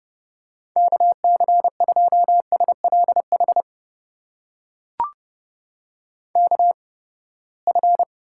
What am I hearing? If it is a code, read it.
Morse code, heard as a keyed tone, about 30 wpm, spaced as for 21 wpm, 700 Hz. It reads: KC2SLH KF
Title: USA Call Signs - 6 Characters - 8x Character Spacing - 30wpm